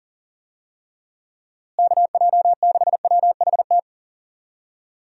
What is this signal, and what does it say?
Morse code, heard as a keyed tone, about 40 wpm, 700 Hz. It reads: KJ6WHT
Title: USA Call Signs - 6 Characters - Repeated 3x - 40wpm